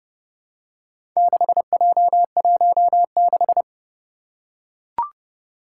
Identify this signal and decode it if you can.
Morse code, heard as a keyed tone, about 30 wpm, 700 Hz. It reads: 6J16
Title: Four Challenging Letter-Number Combinations 30wpm